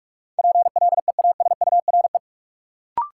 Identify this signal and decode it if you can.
Morse code, heard as a keyed tone, about 41 wpm, 700 Hz. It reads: PLEASURE
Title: Sentences from Top 1600 Words - New Word Drill Alt - 45wpm